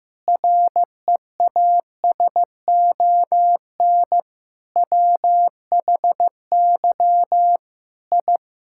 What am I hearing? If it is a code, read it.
Morse code, heard as a keyed tone, about 14 wpm, 700 Hz. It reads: REASON WHY I
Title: Sentences from Top 1000 Words - Review Part 6 - 15wpm